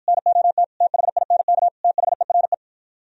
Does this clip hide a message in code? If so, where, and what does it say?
Yes: Morse code from 0.1 to 2.5 s, heard as a keyed tone, about 52 wpm, 700 Hz. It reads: NOT THINK THERE